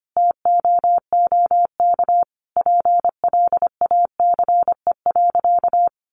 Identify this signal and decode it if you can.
Morse code, heard as a keyed tone, about 25 wpm, 700 Hz. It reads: TOOK PLACE.
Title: Sentences from Top 1500 Words 25wpm